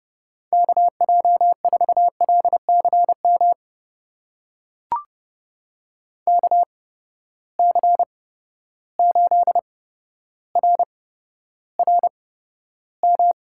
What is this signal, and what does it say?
Morse code, heard as a keyed tone, about 30 wpm, spaced as for 13 wpm, 700 Hz. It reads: KJ4LCM KC8RRM
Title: USA Call Signs - 6 Characters - 8x Character Spacing - 30wpm